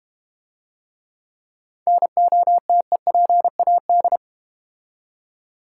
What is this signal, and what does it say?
Morse code, heard as a keyed tone, about 32 wpm, 700 Hz. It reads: NOTEPAD